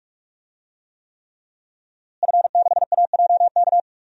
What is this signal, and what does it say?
Morse code, heard as a keyed tone, about 44 wpm, 700 Hz. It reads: F6AJK
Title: DXCC-100 Call Signs - 5 Characters - Repeated 3x - 45wpm